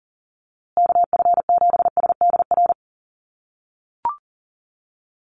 Morse code, heard as a keyed tone, about 37 wpm, 700 Hz. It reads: KF7SDR
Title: USA Call Signs Repeated 2x 40wpm